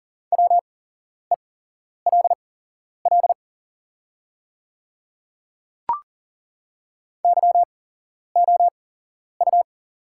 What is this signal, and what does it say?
Morse code, heard as a keyed tone, about 40 wpm, spaced as for 11 wpm, 690 Hz. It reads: WELL YOU
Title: 8x Character Spacing - Top 200 Words - 40wpm